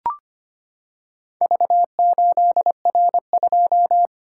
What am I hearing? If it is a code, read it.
Morse code, heard as a keyed tone, about 25 wpm, 700 Hz. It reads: V8R2
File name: practice / 25wpm / Four Letter-Number Combinations - 25wpm